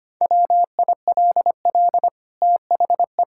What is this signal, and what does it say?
Morse code, heard as a keyed tone, about 25 wpm, 700 Hz. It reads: WILL THE
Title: Sentences from Top 1000 Words - Review Part 6 - 25wpm